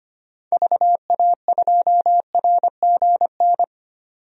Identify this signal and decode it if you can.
Morse code, heard as a keyed tone, about 25 wpm, 700 Hz. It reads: VA2RGN
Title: DXCC-100 Call Signs - 6 Characters - 25wpm